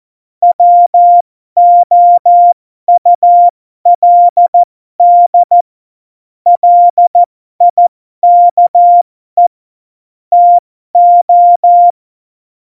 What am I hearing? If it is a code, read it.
Morse code, heard as a keyed tone, about 14 wpm, spaced as for 12 wpm, 700 Hz. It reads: WOULD LIKE TO
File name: Morsekurs / Witze